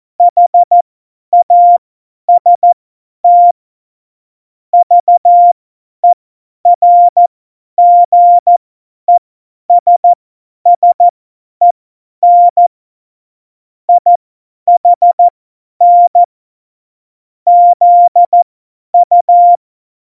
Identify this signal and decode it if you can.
Morse code, heard as a keyed tone, about 14 wpm, spaced as for 10 wpm, 700 Hz. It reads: HAST VERGESSEN IHN ZU
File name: Morsekurs / Witze / Witz 03. Oktober 17, Tempo 50 F70